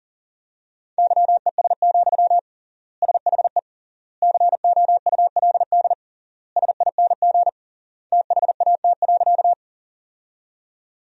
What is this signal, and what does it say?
Morse code, heard as a keyed tone, about 40 wpm, 700 Hz. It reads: YES, SHE COULD SING THAT.
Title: 3x Word-Spacing - Sentences from Top 500 Words - 40wpm